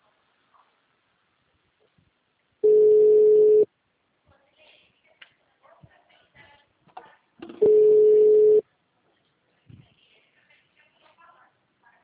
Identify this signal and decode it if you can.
Ringback tone (Europe, 425 Hz).